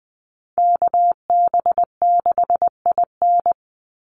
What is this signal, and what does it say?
Morse code, heard as a keyed tone, about 20 wpm, 700 Hz. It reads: KB6IN
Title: USA Call Signs 20wpm